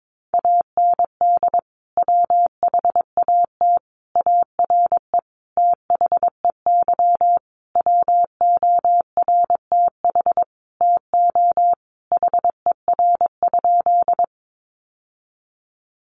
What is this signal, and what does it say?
Morse code, heard as a keyed tone, about 22 wpm, 700 Hz. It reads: AND WHAT ARE THEY WORTH TO HER?
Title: Sentences from Top 800 Words 22wpm